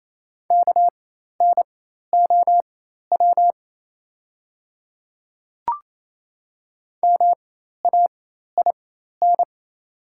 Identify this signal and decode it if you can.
Morse code, heard as a keyed tone, about 28 wpm, spaced as for 13 wpm, 700 Hz. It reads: KNOW MAIN
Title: 4x Character Spacing - Top 300 Words - 28wpm